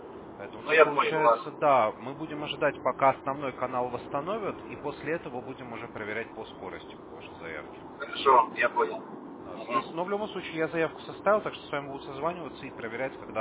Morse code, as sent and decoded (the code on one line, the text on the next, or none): none